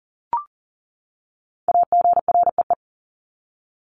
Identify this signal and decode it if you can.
Morse code, heard as a keyed tone, about 40 wpm, 700 Hz. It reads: AGREE
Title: Top 1001-1250 Words 40wpm